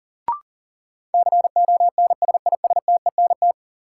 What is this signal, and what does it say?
Morse code, heard as a keyed tone, about 38 wpm, 700 Hz. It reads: CONSISTENT
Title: Sentences from Top 1500 Words - New Word Drill - 40wpm